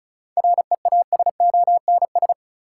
Morse code, heard as a keyed tone, about 36 wpm, 700 Hz. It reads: REASONS
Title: Sets of 4 Words - New Word Drill - 35wpm